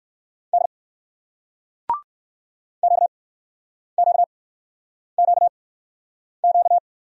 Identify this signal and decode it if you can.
Morse code, heard as a keyed tone, about 59 wpm, spaced as for 9 wpm, 700 Hz. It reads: D QQQQ